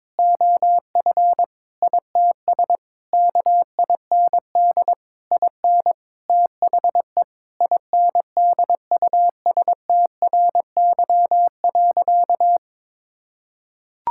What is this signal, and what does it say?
Morse code, heard as a keyed tone, about 22 wpm, 700 Hz. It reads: OF ITS KIND IN THE INDUSTRY.
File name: practice / 22wpm / Sentences with 8-Word Phrases - 22wpm